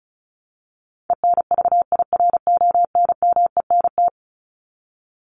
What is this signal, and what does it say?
Morse code, heard as a keyed tone, about 34 wpm, 700 Hz. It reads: ENVIRONMENT
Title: Top 501-750 Words 35wpm